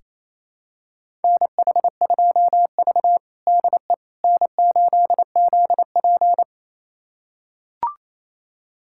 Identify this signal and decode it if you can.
Morse code, heard as a keyed tone, about 28 wpm, 700 Hz. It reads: NH2V DE N8ZP
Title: USA Callsign DE Callsign - 4 Characters - 28wpm